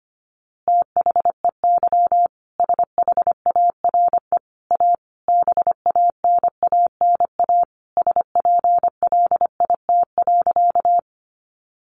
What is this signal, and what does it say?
Morse code, heard as a keyed tone, about 25 wpm, 700 Hz. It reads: THEY SHARE A BANANA SPLIT.
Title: Easy Phrases 25wpm